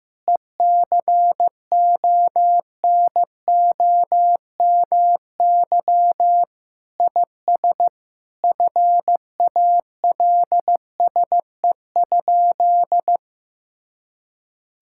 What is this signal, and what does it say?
Morse code, heard as a keyed tone, about 15 wpm, 700 Hz. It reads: ECONOMY IS FALSE?
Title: Sentences from Top 1600 Words 15wpm